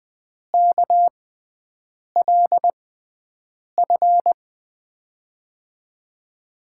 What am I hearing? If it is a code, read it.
Morse code, heard as a keyed tone, about 20 wpm, spaced as for 7 wpm, 700 Hz. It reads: KLF
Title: USA Call Signs - 6 Characters - 6x Character Spacing - 20wpm